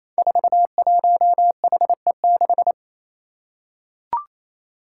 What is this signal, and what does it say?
Morse code, heard as a keyed tone, about 28 wpm, 700 Hz. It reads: 41HE6
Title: Five Challenging Letter-Number Combinations 28wpm